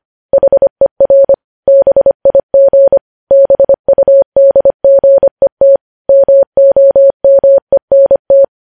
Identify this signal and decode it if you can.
Morse code, heard as a keyed tone, about 25 wpm, 550 Hz. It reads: HER BIG BUDGET MOMENT
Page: News Headlines in Morse Code at 25 WPM